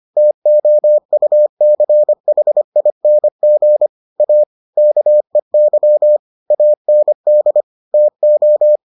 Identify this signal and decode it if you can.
Morse code, heard as a keyed tone, about 25 wpm, 600 Hz. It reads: TOUCHING A KEY AND TO